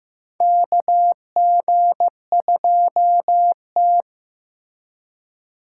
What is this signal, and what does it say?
Morse code, heard as a keyed tone, about 15 wpm, 700 Hz. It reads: KG2T
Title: USA Call Signs - 4 Characters - 3x Character Spacing - 15wpm